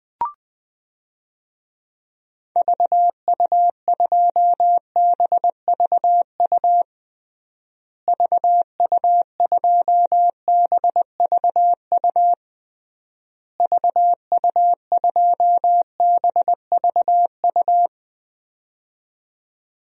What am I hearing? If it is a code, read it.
Morse code, heard as a keyed tone, about 20 wpm, 700 Hz. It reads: VU2BVU VU2BVU VU2BVU
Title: DXCC-100 Call Signs - 6 Characters - Repeated 3x - 20wpm